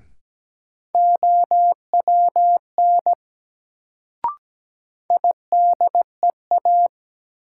Morse code, heard as a keyed tone, about 17 wpm, 700 Hz. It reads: OWN IDEA
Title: Sentences from Top 300 Words - New Word Drill - 17wpm